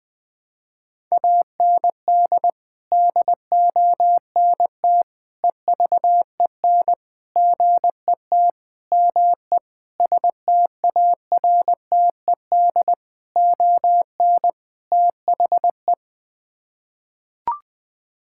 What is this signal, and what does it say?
Morse code, heard as a keyed tone, about 20 wpm, 700 Hz. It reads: AND DONT EVEN GET ME STARTED ON THE
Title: Sets of 8 Words - 20wpm